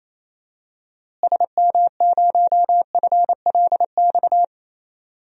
Morse code, heard as a keyed tone, about 28 wpm, 700 Hz. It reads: SM0FLX
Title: DXCC-100 Call Signs - 6 Characters - Repeated 3x - 28wpm